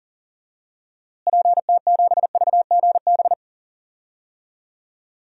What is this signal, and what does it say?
Morse code, heard as a keyed tone, about 37 wpm, 700 Hz. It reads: PT7VGB
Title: DXCC-100 Call Signs - 6 Characters - Repeated 2x - 40wpm